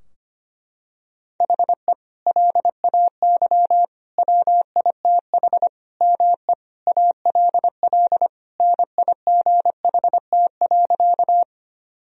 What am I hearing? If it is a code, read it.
Morse code, heard as a keyed tone, about 25 wpm, 700 Hz. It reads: HE LAY WITH ME ALL NIGHT.
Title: Sentences from Top 1000 Words - Review Part 2 - 25wpm